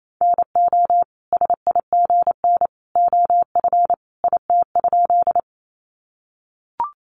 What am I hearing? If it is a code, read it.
Morse code, heard as a keyed tone, about 28 wpm, 700 Hz. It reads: NO SIGN OF IT?
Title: Sentences from Top 1000 Words 28wpm